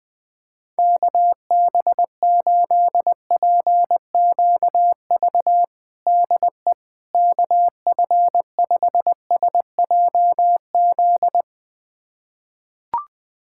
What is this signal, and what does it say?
Morse code, heard as a keyed tone, about 21 wpm, 700 Hz. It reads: KB8PQV DE KF5SJZ
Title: USA Callsign DE Callsign - 6 Characters - 20wpm